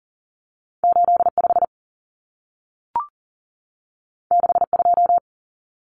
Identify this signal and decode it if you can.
Morse code, heard as a keyed tone, about 40 wpm, 700 Hz. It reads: 85 62
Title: Numbers - 2-Digit 40wpm